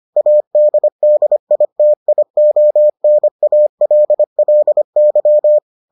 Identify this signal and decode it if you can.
Morse code, heard as a keyed tone, about 25 wpm, 600 Hz. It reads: ADDITIONALLY